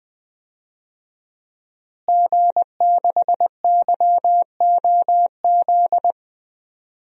Morse code, heard as a keyed tone, about 19 wpm, 700 Hz. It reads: G6YOZ
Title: DXCC-100 Call Signs - 5 Characters - Repeated 2x - 20wpm